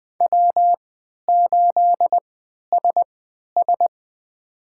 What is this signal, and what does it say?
Morse code, heard as a keyed tone, about 20 wpm, spaced as for 11 wpm, 700 Hz. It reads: W8SS
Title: USA Call Signs - 3x Character Spacing - 20wpm